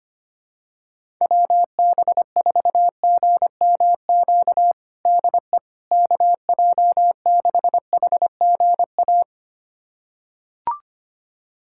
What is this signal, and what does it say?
Morse code, heard as a keyed tone, about 25 wpm, 700 Hz. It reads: WB4GMQ DE KJ6HGA
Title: USA Callsign DE Callsign - 6 Characters - 25wpm